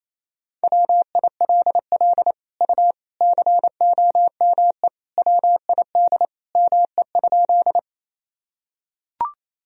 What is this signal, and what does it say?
Morse code, heard as a keyed tone, about 28 wpm, 700 Hz. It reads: WILL U COME WID ME?
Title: Sentences from Top 200 Words and Abbreviations 28wpm